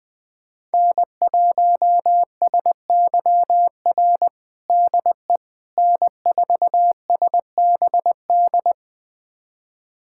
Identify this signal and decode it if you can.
Morse code, heard as a keyed tone, about 20 wpm, 700 Hz. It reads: N1SYR DE N4SBD
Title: USA Callsign DE Callsign - 5 Characters - 20wpm